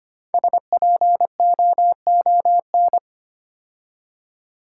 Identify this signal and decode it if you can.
Morse code, heard as a keyed tone, about 25 wpm, 700 Hz. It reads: SPOON
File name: 5 Letter Words - First 3 Letters Given - 25wpm